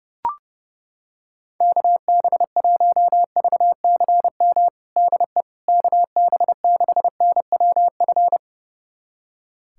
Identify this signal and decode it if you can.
Morse code, heard as a keyed tone, about 30 wpm, 700 Hz. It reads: KB1VCM DE KB6NWF
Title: USA Callsign DE Callsign - 6 Characters - 30wpm